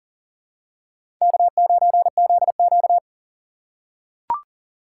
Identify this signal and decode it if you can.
Morse code, heard as a keyed tone, about 40 wpm, 700 Hz. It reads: K9ZQ
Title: USA Call Signs - 3x Character Spacing - 40wpm